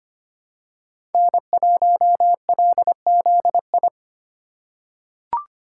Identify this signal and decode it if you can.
Morse code, heard as a keyed tone, about 25 wpm, 700 Hz. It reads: N1LZI
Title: USA Call Signs - 4x Character Spacing - 25wpm